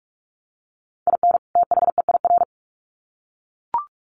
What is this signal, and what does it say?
Morse code, heard as a keyed tone, about 44 wpm, 700 Hz. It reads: IN THEIR
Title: Sets of 2 Words 45wpm